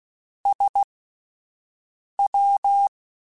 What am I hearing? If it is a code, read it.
Morse code, heard as a keyed tone, about 16 wpm, spaced as for 5 wpm, 800 Hz. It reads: SW
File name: Misc / Misc-mp3s / Jumps @5